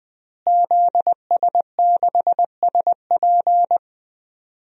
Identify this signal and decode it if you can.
Morse code, heard as a keyed tone, about 20 wpm, 700 Hz. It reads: ZS6SP
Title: DXCC-100 Call Signs - 5 Characters - Repeated 3x - 20wpm